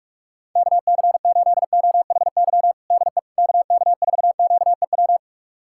Morse code, heard as a keyed tone, about 41 wpm, 700 Hz. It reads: KC8OHY DE KK4QEW
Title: USA Callsign DE Callsign 45wpm